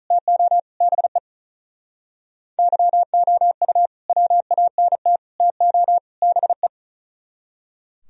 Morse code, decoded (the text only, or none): TO BE YOU WANT TO BE